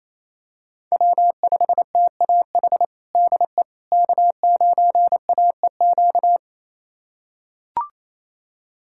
Morse code, heard as a keyed tone, about 28 wpm, 700 Hz. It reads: W5TAH DE K9AEQ
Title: USA Callsign DE Callsign - 5 Characters - 28wpm